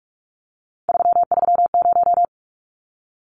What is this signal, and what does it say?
Morse code, heard as a keyed tone, about 44 wpm, 700 Hz. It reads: 330
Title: Numbers - 3-Digit 45wpm